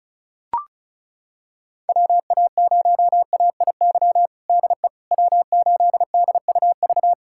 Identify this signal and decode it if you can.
Morse code, heard as a keyed tone, about 36 wpm, 700 Hz. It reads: WA0AIY DE W8DUV
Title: USA Callsign DE Callsign 35wpm